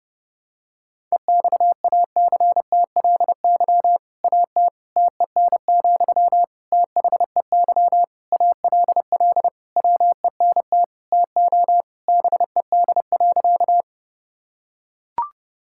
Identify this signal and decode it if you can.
Morse code, heard as a keyed tone, about 30 wpm, 700 Hz. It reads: EXACTLY AT TEN, THEY ALL WENT TO BED.
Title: Sentences from Top 1000 Words - Review Part 6 - 30wpm